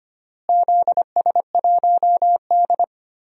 Morse code, heard as a keyed tone, about 25 wpm, 700 Hz. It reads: ZS1D
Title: DXCC-100 Call Signs - 4 Characters - Repeated 2x - 25wpm